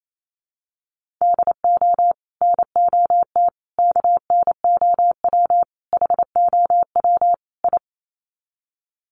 Morse code, heard as a keyed tone, about 28 wpm, 700 Hz. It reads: DO NOT KNOW HOW I
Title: Sets of 5 Words Encore2 28wpm